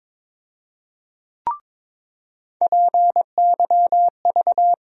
Morse code, heard as a keyed tone, about 22 wpm, 700 Hz. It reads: PYV
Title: USA Call Sign Postfixes - 3 Letters 22wpm